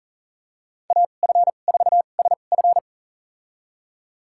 Morse code, heard as a keyed tone, about 40 wpm, spaced as for 29 wpm, 700 Hz. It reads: AF4SF
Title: USA Call Signs - 5 Characters - 2x Character Spacing - 40wpm